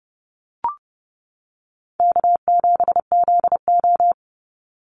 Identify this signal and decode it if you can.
Morse code, heard as a keyed tone, about 30 wpm, 700 Hz. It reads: K7ZO
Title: NA SOTA Chaser Callsigns 30wpm